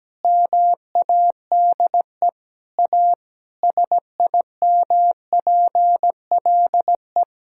Morse code, heard as a keyed tone, about 17 wpm, 700 Hz. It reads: MADE A SIMPLE
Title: Sentences from Top 1000 Words - Review Part 6 - 17wpm